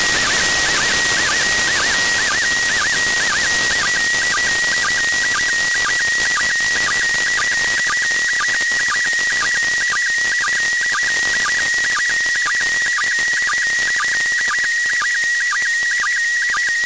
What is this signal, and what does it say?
SSTV (slow-scan television), PD 120.
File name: ISS-SSTV-audio z 03-21-52 13-11-2024